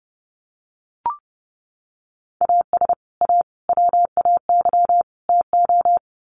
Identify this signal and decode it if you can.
Morse code, heard as a keyed tone, about 30 wpm, 700 Hz. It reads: AS A WAY TO
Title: Sets of 4 Words 30wpm